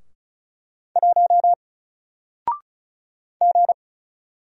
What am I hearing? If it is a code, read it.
Morse code, heard as a keyed tone, about 34 wpm, 700 Hz. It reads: 1 G
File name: Single-Letter-Number-35wpm